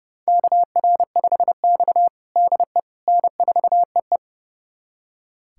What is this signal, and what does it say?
Morse code, heard as a keyed tone, about 32 wpm, 700 Hz. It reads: KR5X DE N4EE
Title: USA Callsign DE Callsign - 4 Characters - 30wpm